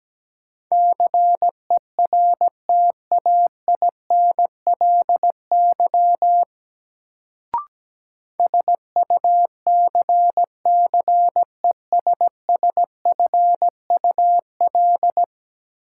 Morse code, heard as a keyed tone, about 17 wpm, 700 Hz. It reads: CERTAINLY SUCCESSFUL
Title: Sentences from Top 900 Words - New Word Drill - 17wpm